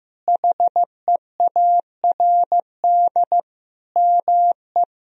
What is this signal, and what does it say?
Morse code, heard as a keyed tone, about 15 wpm, 700 Hz. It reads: HEARD ME
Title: Sentences from Top 1000 Words - Review Part 2 - 15wpm